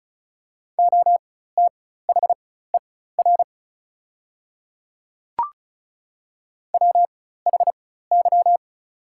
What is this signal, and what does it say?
Morse code, heard as a keyed tone, about 36 wpm, spaced as for 16 wpm, 700 Hz. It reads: OTHER WHY
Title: 4x Character Spacing - Top 200 Words - 35wpm